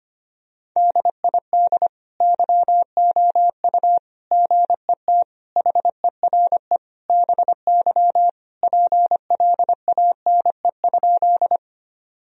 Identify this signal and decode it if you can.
Morse code, heard as a keyed tone, about 25 wpm, 700 Hz. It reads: DID YOU GET HERE BY PLANE?
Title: Sentences from Top 1000 Words - Review Part 3 - 25wpm